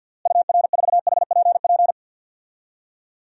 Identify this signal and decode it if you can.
Morse code, heard as a keyed tone, about 51 wpm, 700 Hz. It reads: UR4HPP
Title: DXCC-100 Call Signs - 6 Characters - Repeated 3x - 50wpm